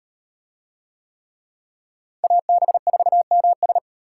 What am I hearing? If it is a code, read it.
Morse code, heard as a keyed tone, about 34 wpm, 700 Hz. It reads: AB4MS